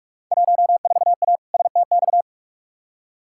Morse code, heard as a keyed tone, about 47 wpm, 700 Hz. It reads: 14A STX
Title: Field Day Exchanges Encore - 45wpm